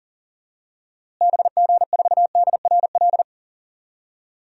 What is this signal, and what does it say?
Morse code, heard as a keyed tone, about 40 wpm, 700 Hz. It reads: BG4DRL
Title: DXCC-100 Call Signs - 6 Characters - Repeated 3x - 40wpm